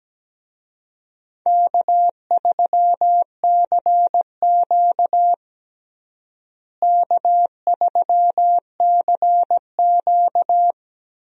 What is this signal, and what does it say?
Morse code, heard as a keyed tone, about 17 wpm, 700 Hz. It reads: K3CQ K3CQ
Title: USA Call Signs - 4 Characters - Repeated 3x - 17wpm